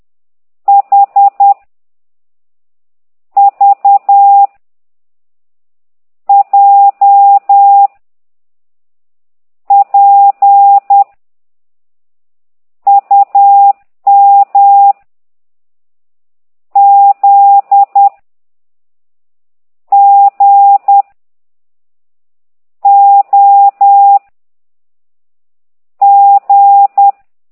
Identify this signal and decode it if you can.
Morse code, heard as a keyed tone, about 10 wpm, 800 Hz. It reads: H V J P U M Z G O G